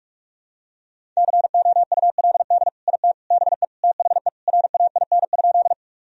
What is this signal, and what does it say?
Morse code, heard as a keyed tone, about 46 wpm, 700 Hz. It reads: COULD IT BE THE RAIN?
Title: Sentences from Top 1000 Words - Review Part 1 - 45wpm